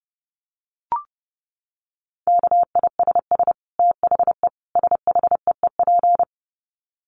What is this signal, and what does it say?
Morse code, heard as a keyed tone, about 30 wpm, 700 Hz. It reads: KISS THE SHEEP
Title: Dit Challenge 30wpm